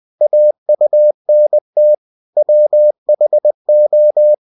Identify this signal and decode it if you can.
Morse code, heard as a keyed tone, about 20 wpm, 600 Hz. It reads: AUNT WHO
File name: cw-books / winnie-the-pooh / winnie-20wpm-page-0101